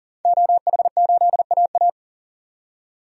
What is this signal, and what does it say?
Morse code, heard as a keyed tone, about 40 wpm, 700 Hz. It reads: OH8AA
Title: DXCC-100 Call Signs - 5 Characters - 40wpm